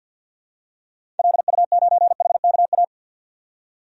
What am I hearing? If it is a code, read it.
Morse code, heard as a keyed tone, about 50 wpm, 700 Hz. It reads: LU9HKA
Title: DXCC-100 Call Signs - 6 Characters - 50wpm